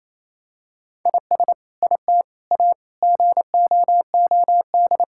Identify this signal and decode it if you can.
Morse code, heard as a keyed tone, about 28 wpm, 700 Hz. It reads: IS IT A GOOD
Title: Sentences from Top 1000 Words - Review Part 5 - 28wpm